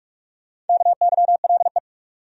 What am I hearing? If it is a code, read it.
Morse code, heard as a keyed tone, about 42 wpm, 700 Hz. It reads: KYLE
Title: USA Common Names - Max 4 Letters - 45wpm